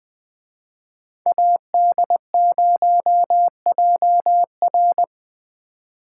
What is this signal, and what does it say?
Morse code, heard as a keyed tone, about 24 wpm, 700 Hz. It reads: AD0JR
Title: Speed Racing - USA 2x2 Call Signs 20wpm